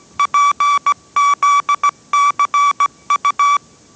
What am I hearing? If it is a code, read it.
Morse code, heard as a keyed tone, about 17 wpm, 1150 Hz. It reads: PZCU
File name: 18wpm